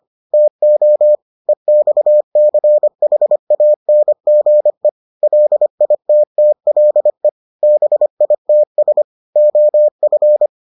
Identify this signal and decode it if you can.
Morse code, heard as a keyed tone, about 25 wpm, 600 Hz. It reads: TO EXCHANGE LITTLE BITS OF